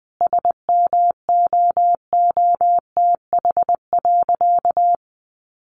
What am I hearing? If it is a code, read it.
Morse code, heard as a keyed tone, about 19 wpm, 700 Hz. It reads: SMOOTH.
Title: Sentences from Top 1400 Words 20wpm